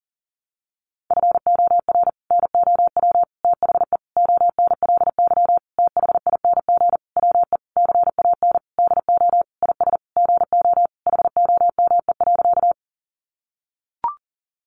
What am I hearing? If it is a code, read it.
Morse code, heard as a keyed tone, about 40 wpm, 700 Hz. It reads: FOR NOW THE ONLY THING WE CAN DO IS GO HOME.
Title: Sentences from Top 200 Words 40wpm